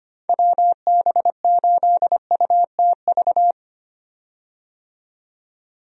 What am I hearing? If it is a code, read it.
Morse code, heard as a keyed tone, about 25 wpm, 700 Hz. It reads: WB8UTV
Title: USA Call Signs - 6 Characters - 25wpm